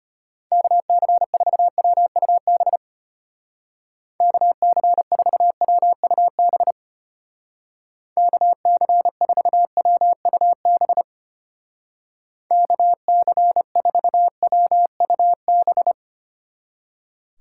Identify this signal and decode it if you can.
Morse code, heard as a keyed tone, about 33 wpm, 700 Hz. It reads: KC4WUB KC4WUB KC4WUB KC4WUB